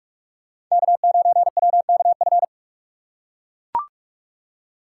Morse code, heard as a keyed tone, about 44 wpm, 700 Hz. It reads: K9WKF
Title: USA Call Signs - 5 Characters - 6x Character Spacing - 45wpm